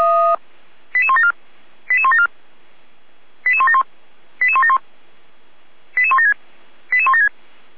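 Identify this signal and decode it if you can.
Five-tone selective call (ZVEI), heard as quick runs of five tones, tones of about 70 ms. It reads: 88174 88174 88171 88171 88176 88176
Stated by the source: ZVEI five-tone selective call received by SDR.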